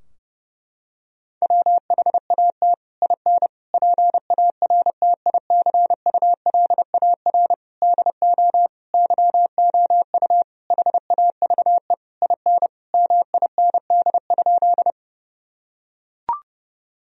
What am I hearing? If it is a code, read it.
Morse code, heard as a keyed tone, about 30 wpm, 700 Hz. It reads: WHAT IN PARTICULAR DO YOU HAVE IN MIND?
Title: Sentences from Top 1000 Words - Review Part 4 - 30wpm